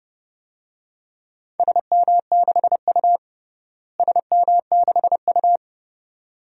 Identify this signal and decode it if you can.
Morse code, heard as a keyed tone, about 32 wpm, 700 Hz. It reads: SM6U SM6U
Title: DXCC-100 Call Signs - 4 Characters - Repeated 2x - 30wpm